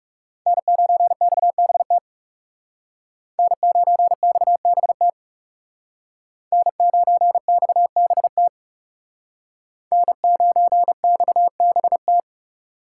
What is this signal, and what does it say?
Morse code, heard as a keyed tone, about 39 wpm, 700 Hz. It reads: N9XBT N9XBT N9XBT N9XBT